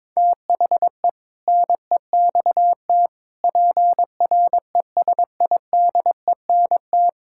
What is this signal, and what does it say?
Morse code, heard as a keyed tone, about 22 wpm, 700 Hz. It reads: THE NEXT PRESIDENT
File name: practice / 22wpm / Sets of 7 Words - 22wpm